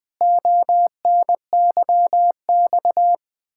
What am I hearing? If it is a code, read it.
Morse code, heard as a keyed tone, about 21 wpm, 700 Hz. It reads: ONYX
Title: Pangram Challenge v2 - 20wpm